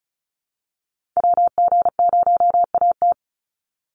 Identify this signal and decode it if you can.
Morse code, heard as a keyed tone, about 34 wpm, 700 Hz. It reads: WG0AT
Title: NA SOTA Chaser Callsigns 35wpm